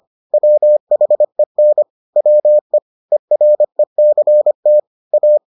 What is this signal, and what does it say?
Morse code, heard as a keyed tone, about 25 wpm, 600 Hz. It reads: WHEN WE ERECT A